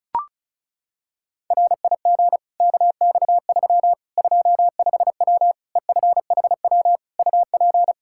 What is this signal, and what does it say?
Morse code, heard as a keyed tone, about 36 wpm, 700 Hz. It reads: RIG KX3 25W EFHW UP
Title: Practice QSO - Exchange 2 - Info - 35wpm